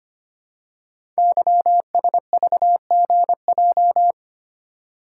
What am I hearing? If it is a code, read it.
Morse code, heard as a keyed tone, about 25 wpm, 700 Hz. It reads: YSVGJ
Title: Code Groups - 5 Letters - 25wpm